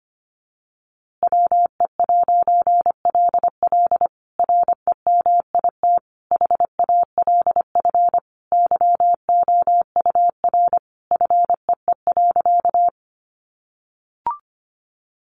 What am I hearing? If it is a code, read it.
Morse code, heard as a keyed tone, about 25 wpm, 700 Hz. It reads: WE'LL REMIT HALF YOUR FEE.
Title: A Sound Of Thunder 25wpm